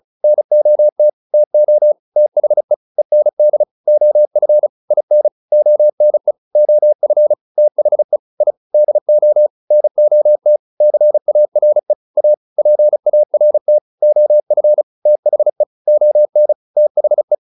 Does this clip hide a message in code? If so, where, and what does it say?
Yes: Morse code from 0.2 to 17.4 s, heard as a keyed tone, about 34 wpm, 600 Hz. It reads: NOT TO THE END OF IN ONE OF THE I DO NOT CARE A PART OF THE ON THE